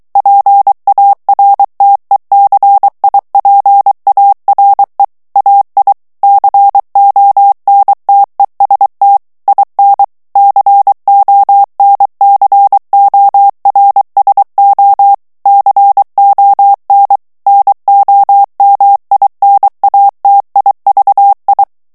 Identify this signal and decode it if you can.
Morse code, heard as a keyed tone, about 23 wpm, 800 Hz. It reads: PARTECIPARE AI CONTEST IN CONCORSO CON NOMINATIVI
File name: ITALIEN texte clair 2 vitesse 22 mots minute